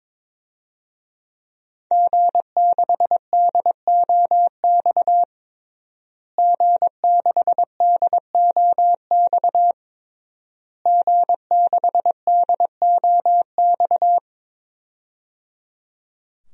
Morse code, heard as a keyed tone, about 22 wpm, 700 Hz. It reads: G6DOX G6DOX G6DOX
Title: DXCC-100 Call Signs - 5 Characters - Repeated 3x - 22wpm